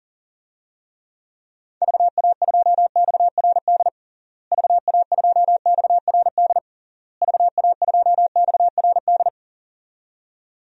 Morse code, heard as a keyed tone, about 40 wpm, 700 Hz. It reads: VA2XRD VA2XRD VA2XRD
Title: DXCC-100 Call Signs - 6 Characters - Repeated 3x - 40wpm